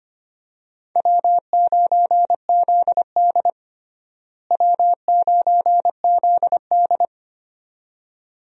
Morse code, heard as a keyed tone, about 25 wpm, 700 Hz. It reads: W9ZD W9ZD
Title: USA Call Signs - 4 Characters - Repeated 2x - 25wpm